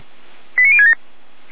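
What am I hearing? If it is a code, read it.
Five-tone selective call (ZVEI), heard as quick runs of five tones, tones of about 70 ms. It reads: 89067
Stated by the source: ZVEI five-tone call 89078 received by SDR, archived by feuerwehr-orlamuende (Orlamünde fire brigade).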